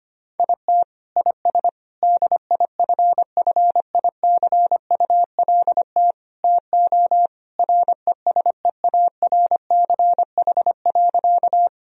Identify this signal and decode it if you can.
Morse code, heard as a keyed tone, about 25 wpm, 700 Hz. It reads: IT IS DIFFICULT TO RESEARCH.
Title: Sentences from Top 1000 Words - Review Part 4 - 25wpm